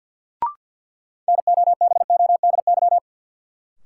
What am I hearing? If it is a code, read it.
Morse code, heard as a keyed tone, about 48 wpm, 700 Hz. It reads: NOBODY
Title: Sentences from Top 1600 Words - New Word Drill Alt - 50wpm